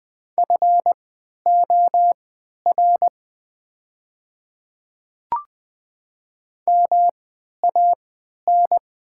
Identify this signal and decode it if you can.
Morse code, heard as a keyed tone, about 20 wpm, spaced as for 11 wpm, 700 Hz. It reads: FOR MAN
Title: 3x Character Spacing - Top 300 Words - 20wpm